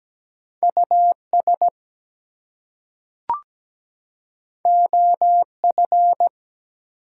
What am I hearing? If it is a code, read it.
Morse code, heard as a keyed tone, about 17 wpm, 700 Hz. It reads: US OF
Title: Two Letter Words - 17wpm